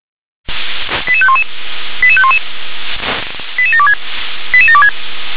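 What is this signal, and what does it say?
Five-tone selective call (ZVEI), heard as quick runs of five tones, tones of about 80 ms. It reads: 88511 88511 88726 88726
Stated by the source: Five-tone ZVEI selective call received on an SDR.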